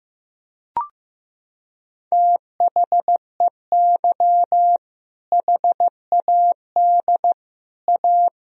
Morse code, heard as a keyed tone, about 15 wpm, 700 Hz. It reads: THEY HAD A
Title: Sentences from Top 1000 Words - Review Part 2 - 15wpm